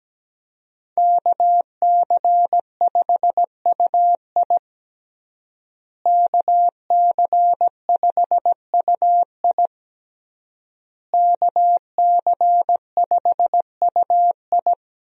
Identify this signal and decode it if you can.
Morse code, heard as a keyed tone, about 17 wpm, 700 Hz. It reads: KC5UI KC5UI KC5UI
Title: USA Call Signs - 5 Characters - Repeated 3x - 17wpm